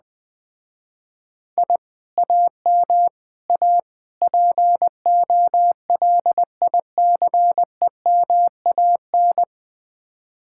Morse code, heard as a keyed tone, about 20 wpm, 700 Hz. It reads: I AM A POLICEMAN
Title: Speed-Racing Common QSO Phrases - 17wpm